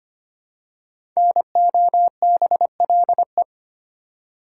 Morse code, heard as a keyed tone, about 25 wpm, 700 Hz. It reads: NOBLE
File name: 5 Letter Words - First 3 Letters Given - 25wpm